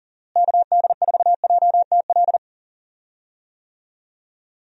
Morse code, heard as a keyed tone, about 40 wpm, 700 Hz. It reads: KD4JTL
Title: USA Call Signs - 6 Characters - Repeated 2x - 40wpm